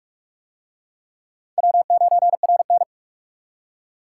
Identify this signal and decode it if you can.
Morse code, heard as a keyed tone, about 44 wpm, 700 Hz. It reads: W9RN